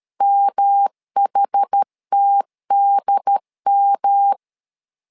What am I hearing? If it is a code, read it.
Morse code, heard as a keyed tone, about 13 wpm, 800 Hz. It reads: MHTDM
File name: LEZIONI / 4 Q1 12wpm 2028